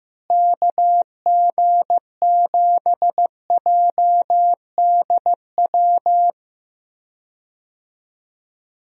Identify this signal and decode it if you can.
Morse code, heard as a keyed tone, about 15 wpm, 700 Hz. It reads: KG7JDW
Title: USA Call Signs - 6 Characters - Repeated 2x - 15wpm